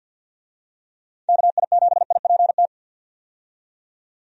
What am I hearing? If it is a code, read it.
Morse code, heard as a keyed tone, about 53 wpm, 700 Hz. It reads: KI7IPT
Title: USA Call Signs - 6 Characters - 50wpm